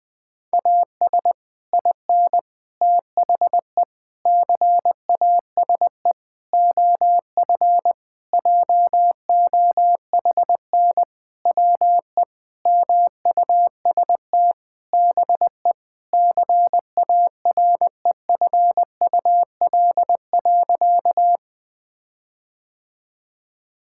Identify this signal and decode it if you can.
Morse code, heard as a keyed tone, about 20 wpm, 700 Hz. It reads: AS IN THE CASE OF JOHN WE MUST BE CAREFUL.
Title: Sentences with 5-Word Phrases - Spoken Beginning - 20wpm